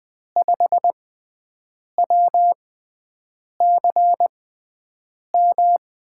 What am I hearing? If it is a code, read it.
Morse code, heard as a keyed tone, about 20 wpm, spaced as for 7 wpm, 700 Hz. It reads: 5WCM
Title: USA Call Signs - 6 Characters - 6x Character Spacing - 20wpm